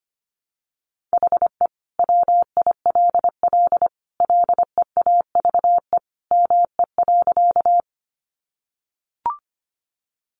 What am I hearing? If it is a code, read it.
Morse code, heard as a keyed tone, about 25 wpm, 700 Hz. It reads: HE WILL LEAVE ME.
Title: Sentences from Top 400 Words 25wpm